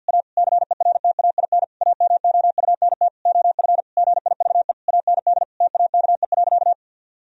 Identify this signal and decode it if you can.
Morse code, heard as a keyed tone, about 50 wpm, 700 Hz. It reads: A CERTAIN AMOUNT OF GIVE AND TAKE.
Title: Sentences from Top 1000 Words - Review Part 4 - 50wpm